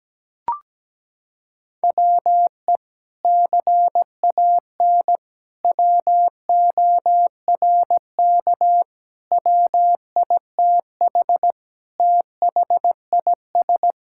Morse code, heard as a keyed tone, about 17 wpm, 700 Hz. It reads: WE CAN WORK WITH THIS
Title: Sentences from Top 1000 Words - Review Part 5 - 17wpm